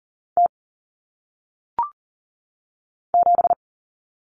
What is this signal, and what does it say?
Morse code, heard as a keyed tone, about 40 wpm, 700 Hz. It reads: T 7
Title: Common Cut Numbers - 1-Digit 40wpm